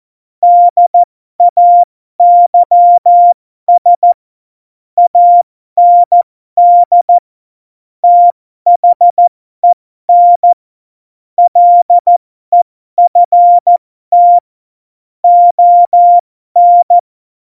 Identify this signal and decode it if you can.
Morse code, heard as a keyed tone, about 14 wpm, spaced as for 12 wpm, 700 Hz. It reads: DAYS AND THEN LEFT ON
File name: Morsekurs / Witze